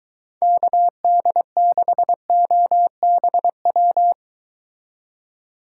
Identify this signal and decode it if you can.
Morse code, heard as a keyed tone, about 23 wpm, 700 Hz. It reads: KD6OBW